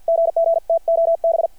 Morse code, heard as a keyed tone, about 50 wpm, 660 Hz. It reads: GG TO B
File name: archived / CRSnet